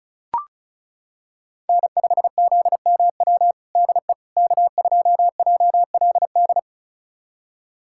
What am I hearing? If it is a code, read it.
Morse code, heard as a keyed tone, about 36 wpm, 700 Hz. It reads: N5ZMW DE K2JLD
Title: USA Callsign DE Callsign - 5 Characters - 35wpm